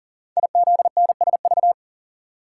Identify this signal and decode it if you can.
Morse code, heard as a keyed tone, about 43 wpm, 700 Hz. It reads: IZNSV
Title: Code Groups - 5 Letters - 40wpm